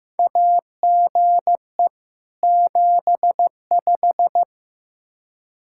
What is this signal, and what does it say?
Morse code, heard as a keyed tone, about 15 wpm, 700 Hz. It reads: AGE 75
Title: Practice QSO - Exchange 3 - More Info - 15wpm